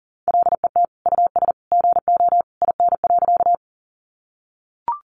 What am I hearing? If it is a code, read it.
Morse code, heard as a keyed tone, about 40 wpm, 700 Hz. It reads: LET US GO IN.
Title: Sentences from Top 300 Words 40wpm